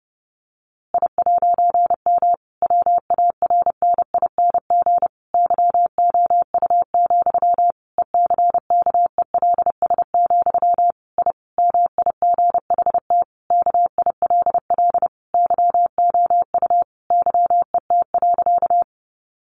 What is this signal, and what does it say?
Morse code, heard as a keyed tone, about 30 wpm, 700 Hz. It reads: I'M WARNING YOU, ECKELS, I MIGHT KILL YOU YET.
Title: A Sound Of Thunder 30wpm